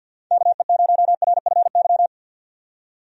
Morse code, heard as a keyed tone, about 51 wpm, 700 Hz. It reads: KE0RFY
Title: USA Call Signs - 6 Characters - 50wpm